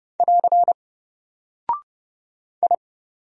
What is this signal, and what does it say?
Morse code, heard as a keyed tone, about 29 wpm, 700 Hz. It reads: <AR> I
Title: Single Character (Letter:Number:Punctuation:Prosign) 30wpm